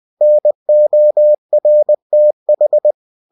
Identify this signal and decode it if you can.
Morse code, heard as a keyed tone, about 20 wpm, 600 Hz. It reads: NORTH
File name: cw-books / winnie-the-pooh / winnie-20wpm-page-0080